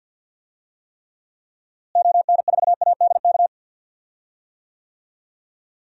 Morse code, heard as a keyed tone, about 46 wpm, 700 Hz. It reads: ON4ADK
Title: DXCC-100 Call Signs - 6 Characters - Repeated 3x - 50wpm